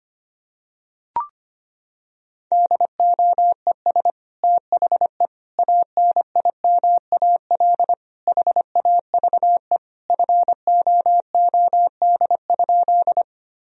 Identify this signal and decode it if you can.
Morse code, heard as a keyed tone, about 25 wpm, 700 Hz. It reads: DOES THE ANIMAL HAVE FOOD?
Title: Sentences from Top 1000 Words - Review Part 1 - 25wpm